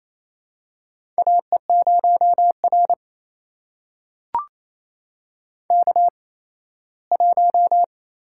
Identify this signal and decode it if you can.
Morse code, heard as a keyed tone, about 27 wpm, spaced as for 17 wpm, 700 Hz. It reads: AE0R K1
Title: USA Call Signs - 4 Characters - 8x Character Spacing - 28wpm